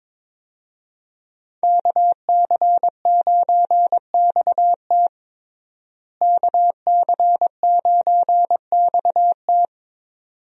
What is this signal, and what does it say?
Morse code, heard as a keyed tone, about 22 wpm, 700 Hz. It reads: KC9XT KC9XT
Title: USA Call Signs - 5 Characters - Repeated 2x - 22wpm